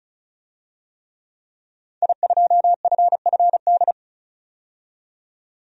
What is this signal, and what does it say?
Morse code, heard as a keyed tone, about 34 wpm, 700 Hz. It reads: I2FFD
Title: DXCC-100 Call Signs - 5 Characters - 35wpm